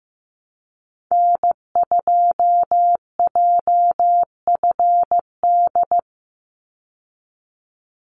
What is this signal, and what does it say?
Morse code, heard as a keyed tone, about 15 wpm, 700 Hz. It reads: N2JFD
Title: Call Signs Repeated 3x - 15wpm